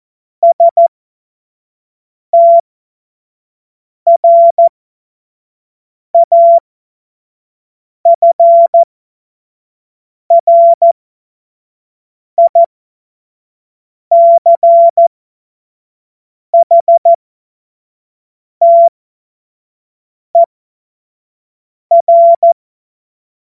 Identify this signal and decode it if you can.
Morse code, heard as a keyed tone, about 14 wpm, spaced as for 5 wpm, 680 Hz. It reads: STRAFRICHTER
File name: Morsekurs / Witze / Witz 23. August 2022 25 F70